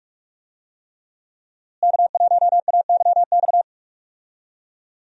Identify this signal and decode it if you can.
Morse code, heard as a keyed tone, about 46 wpm, 700 Hz. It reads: K1AYX
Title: USA Call Signs - 5 Characters - Repeated 3x - 45wpm